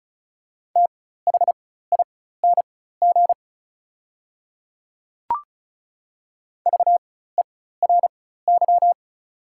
Morse code, heard as a keyed tone, about 36 wpm, spaced as for 16 wpm, 690 Hz. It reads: THING VERY
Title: 4x Character Spacing - Top 300 Words - 35wpm